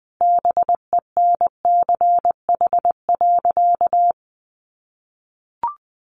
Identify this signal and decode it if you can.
Morse code, heard as a keyed tone, about 20 wpm, 700 Hz. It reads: BENCH.
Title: Sentences from Top 1400 Words 20wpm